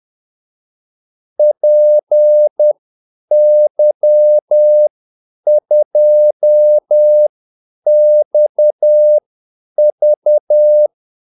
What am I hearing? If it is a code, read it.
Morse code, heard as a keyed tone, about 10 wpm, spaced as for 8 wpm, 600 Hz. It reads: PY2XV
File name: Callsigns-10-8wpm